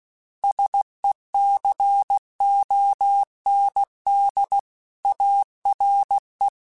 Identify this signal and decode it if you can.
Morse code, heard as a keyed tone, about 16 wpm, 800 Hz. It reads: SECOND ARE